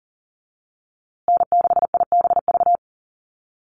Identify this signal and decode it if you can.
Morse code, heard as a keyed tone, about 40 wpm, 700 Hz. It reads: N6IBV
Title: USA Call Signs Repeated 2x 40wpm